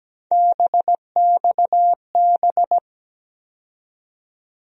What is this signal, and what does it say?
Morse code, heard as a keyed tone, about 17 wpm, 700 Hz. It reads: BXB
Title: USA Call Sign Postfixes - 3 Letters 17wpm